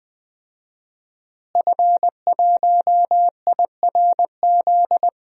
Morse code, heard as a keyed tone, about 20 wpm, 700 Hz. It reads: F1IRZ
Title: DXCC-100 Call Signs - 5 Characters - Repeated 3x - 20wpm